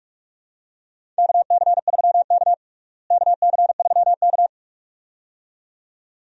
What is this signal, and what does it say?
Morse code, heard as a keyed tone, about 43 wpm, 700 Hz. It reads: KC3K KC3K
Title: USA Call Signs - 4 Characters - Repeated 2x - 45wpm